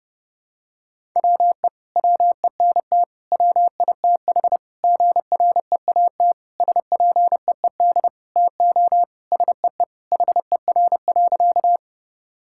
Morse code, heard as a keyed tone, about 30 wpm, 700 Hz. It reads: WE WENT WITH GREAT SPEED TO SEE HER.
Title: Sentences from Top 1000 Words - Review Part 5 - 30wpm